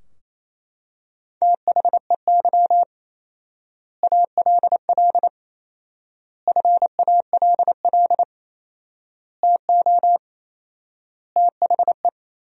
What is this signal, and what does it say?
Morse code, heard as a keyed tone, about 28 wpm, 700 Hz. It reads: THEY ALL FALL TO THE
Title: 4x Word-Spacing - Sentences from Top 500 Words - 28wpm